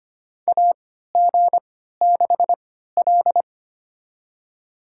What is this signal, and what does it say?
Morse code, heard as a keyed tone, about 25 wpm, spaced as for 14 wpm, 700 Hz. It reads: AG6L
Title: USA Call Signs - 3x Character Spacing - 25wpm